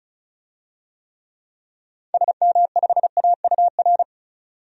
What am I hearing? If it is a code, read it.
Morse code, heard as a keyed tone, about 36 wpm, 700 Hz. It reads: SM5AUR
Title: DXCC-100 Call Signs - 6 Characters - 35wpm